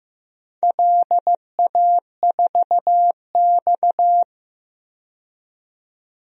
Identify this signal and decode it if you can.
Morse code, heard as a keyed tone, about 16 wpm, 700 Hz. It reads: LA4X
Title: DXCC-100 Call Signs - 4 Characters - 15wpm